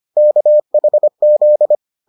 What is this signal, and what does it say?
Morse code, heard as a keyed tone, about 25 wpm, 600 Hz. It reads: KHZ